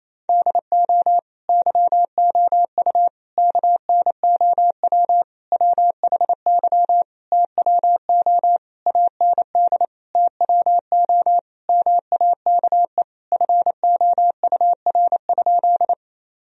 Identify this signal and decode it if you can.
Morse code, heard as a keyed tone, about 28 wpm, 700 Hz. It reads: DO YOU KNOW WHY TWO AND TWO MAKE FOUR?
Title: Sentences from Top 1000 Words - Review Part 1 - 28wpm